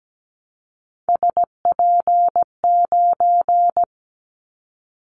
Morse code, heard as a keyed tone, about 17 wpm, 700 Hz. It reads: SP9
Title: Callsign Prefixes for Top 25 DXCC - Repeated 3x - 17wpm